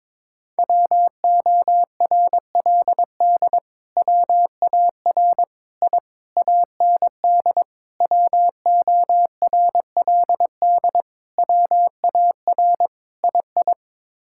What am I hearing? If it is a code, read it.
Morse code, heard as a keyed tone, about 22 wpm, 700 Hz. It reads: WORLD WAR I AND WORLD WAR II
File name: practice / 22wpm / Sets of 7 Words - 22wpm